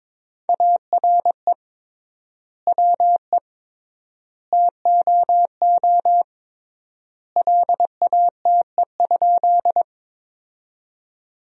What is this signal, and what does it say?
Morse code, heard as a keyed tone, about 21 wpm, 700 Hz. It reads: ARE WE TOO LATE?
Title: 3x Word-Spacing - Sentences from Top 300 Words - 22wpm